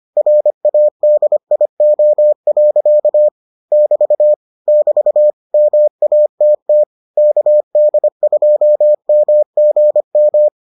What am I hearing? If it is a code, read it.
Morse code, heard as a keyed tone, about 25 wpm, 600 Hz. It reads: RADIO. <BT> <BT> MATT KD2MGM